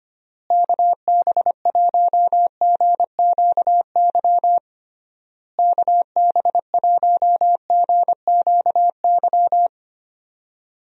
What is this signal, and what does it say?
Morse code, heard as a keyed tone, about 25 wpm, 700 Hz. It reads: KB1GQY KB1GQY
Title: USA Call Signs - 6 Characters - Repeated 3x - 25wpm